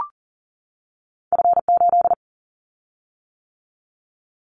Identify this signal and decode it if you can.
Morse code, heard as a keyed tone, about 42 wpm, 700 Hz. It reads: F8
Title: Callsign Prefixes for Top 25 DXCC - 40wpm